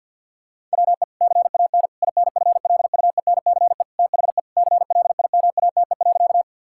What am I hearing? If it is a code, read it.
Morse code, heard as a keyed tone, about 50 wpm, 700 Hz. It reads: WE CAN INFLUENCE THE CLIMATE.
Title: Sentences from Top 1600 Words 50wpm